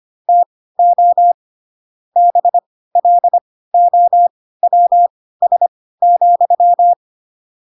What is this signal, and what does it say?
Morse code, heard as a keyed tone, about 25 wpm, spaced as for 16 wpm, 700 Hz. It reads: TO BLOWS,
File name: Pinocchio / Pinocchio ch2 10 25 ik1hge